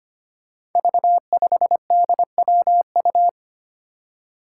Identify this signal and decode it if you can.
Morse code, heard as a keyed tone, about 25 wpm, 700 Hz. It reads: V5DWU
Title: Five Challenging Letter-Number Combinations 25wpm